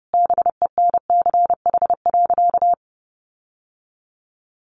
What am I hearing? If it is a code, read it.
Morse code, heard as a keyed tone, about 28 wpm, 700 Hz. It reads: BENCH.
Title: Sentences from Top 1400 Words 30wpm